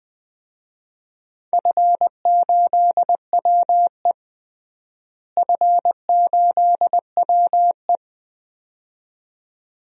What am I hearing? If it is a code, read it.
Morse code, heard as a keyed tone, about 20 wpm, 700 Hz. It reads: F8WE F8WE
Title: DXCC-100 Call Signs - 4 Characters - Repeated 2x - 20wpm